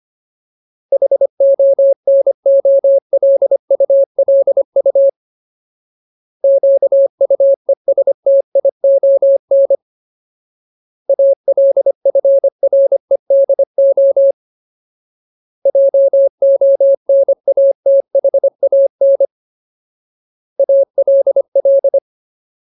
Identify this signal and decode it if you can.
Morse code, heard as a keyed tone, about 25 wpm, 550 Hz. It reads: HONOLULU QUESTION ALFREDO JONATHAN ALL
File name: Ultimate Word List / 25WPM Ultimate